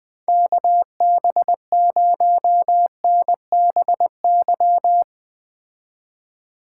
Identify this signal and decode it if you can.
Morse code, heard as a keyed tone, about 20 wpm, 700 Hz. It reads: KB0NBY